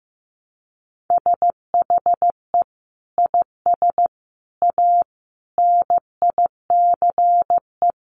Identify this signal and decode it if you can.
Morse code, heard as a keyed tone, about 15 wpm, 700 Hz. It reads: SHE IS A NICE
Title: Sentences from Top 1100 Words 15wpm